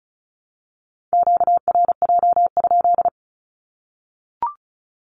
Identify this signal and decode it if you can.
Morse code, heard as a keyed tone, about 34 wpm, 700 Hz. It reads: QRJ?
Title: Q-Codes Repeated 3x 35wpm